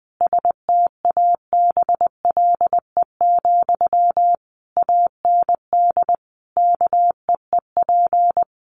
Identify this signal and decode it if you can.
Morse code, heard as a keyed tone, about 20 wpm, 700 Hz. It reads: STABLE, AND KEEP
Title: Sentences from Top 1500 Words 20wpm